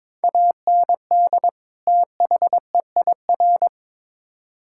Morse code, heard as a keyed tone, about 22 wpm, 700 Hz. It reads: AND THEIR
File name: Build A Phrase - 2-Word Phrases - 22wpm